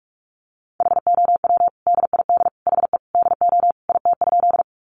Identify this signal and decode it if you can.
Morse code, heard as a keyed tone, about 44 wpm, 700 Hz. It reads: HOW DID HE DO IT?
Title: Sentences from Top 100 Words 45wpm